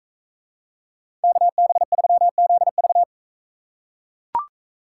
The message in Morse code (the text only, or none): KB3ZV